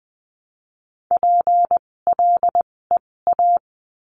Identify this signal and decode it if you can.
Morse code, heard as a keyed tone, about 20 wpm, spaced as for 16 wpm, 700 Hz. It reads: PLEA
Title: 256 Lesson 42a - Words 12-20wpm